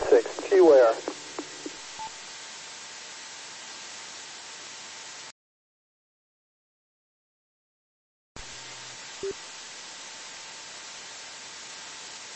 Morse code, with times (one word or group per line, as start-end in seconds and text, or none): none